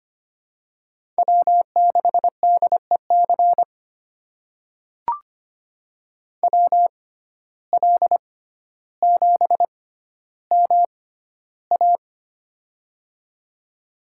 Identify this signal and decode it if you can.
Morse code, heard as a keyed tone, about 25 wpm, spaced as for 13 wpm, 700 Hz. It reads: W6DEC WL7MA
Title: USA Call Signs - 5 Characters - 6x Character Spacing - 25wpm